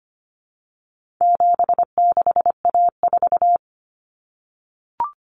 Morse code, heard as a keyed tone, about 25 wpm, 700 Hz. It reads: 76A4
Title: Common Cut Numbers - 4-Digit 25wpm